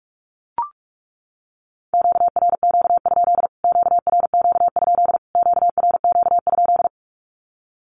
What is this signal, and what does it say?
Morse code, heard as a keyed tone, about 44 wpm, 700 Hz. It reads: QRQ? QRQ? QRQ?
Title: Q-Codes Repeated 3x 45wpm